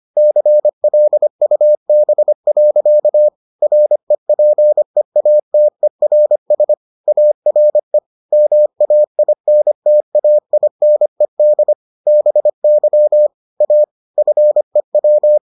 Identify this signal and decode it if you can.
Morse code, heard as a keyed tone, about 25 wpm, 600 Hz. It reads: CLUB. REPEATERS ARE MAINTAINED BY A FEW